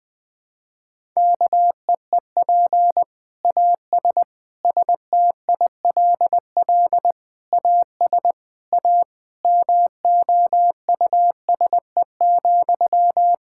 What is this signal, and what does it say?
Morse code, heard as a keyed tone, about 20 wpm, 700 Hz. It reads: KEEP AS STILL AS A MOUSE,
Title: Sentences from Top 1000 Words - Review Part 5 - 20wpm